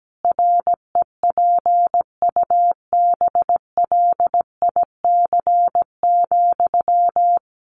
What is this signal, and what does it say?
Morse code, heard as a keyed tone, about 17 wpm, 700 Hz. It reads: REPUBLIC,
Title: Sentences from Top 1500 Words 17wpm